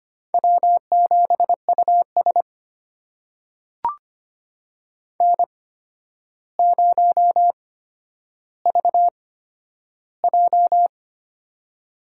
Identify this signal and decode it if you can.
Morse code, heard as a keyed tone, about 25 wpm, spaced as for 11 wpm, 700 Hz. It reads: W7US N0VJ
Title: USA Call Signs - 4 Characters - 8x Character Spacing - 25wpm